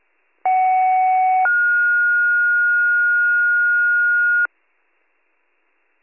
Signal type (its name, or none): two-tone page